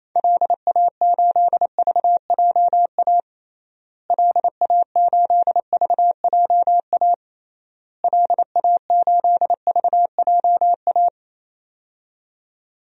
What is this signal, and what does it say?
Morse code, heard as a keyed tone, about 28 wpm, 700 Hz. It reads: LA8VJA LA8VJA LA8VJA
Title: DXCC-100 Call Signs - 6 Characters - Repeated 3x - 28wpm